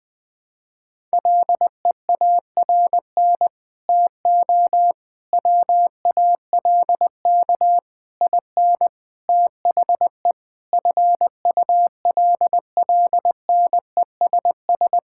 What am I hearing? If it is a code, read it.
Morse code, heard as a keyed tone, about 21 wpm, 700 Hz. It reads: LEARN TO WALK IN THE FULLNESS
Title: Sets of 9 Words - 20wpm